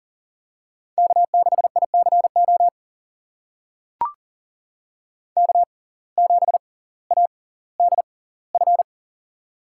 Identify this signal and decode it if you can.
Morse code, heard as a keyed tone, about 40 wpm, spaced as for 20 wpm, 700 Hz. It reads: K6ICO K7ADF
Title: USA Call Signs - 5 Characters - 6x Character Spacing - 40wpm